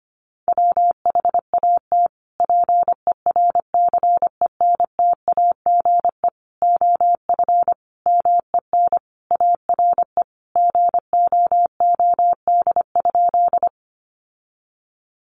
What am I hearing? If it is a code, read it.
Morse code, heard as a keyed tone, about 25 wpm, 700 Hz. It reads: WHAT PERCENTAGE OF MEN ARE GOOD?
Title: Sentences from Top 1300 Words 25wpm